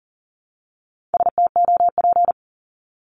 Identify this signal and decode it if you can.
Morse code, heard as a keyed tone, about 40 wpm, 700 Hz. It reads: STOP
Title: Top 500 Words 40wpm